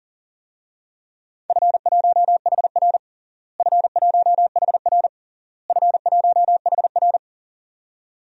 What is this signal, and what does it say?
Morse code, heard as a keyed tone, about 40 wpm, 700 Hz. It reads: F1HR F1HR F1HR
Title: DXCC-100 Call Signs - 4 Characters - Repeated 3x - 40wpm